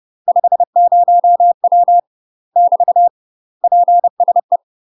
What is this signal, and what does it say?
Morse code, heard as a keyed tone, about 30 wpm, 700 Hz. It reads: 50W <BT> PSE
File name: UK MASTER QSOs TRAINING / 30WPM / UK MASTER QSOs 0048